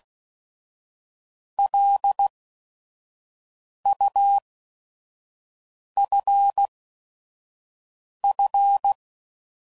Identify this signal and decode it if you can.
Morse code, heard as a keyed tone, about 16 wpm, spaced as for 5 wpm, 800 Hz. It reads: LUFF
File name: CW Code Groups @5WPM / fromSM's / Cities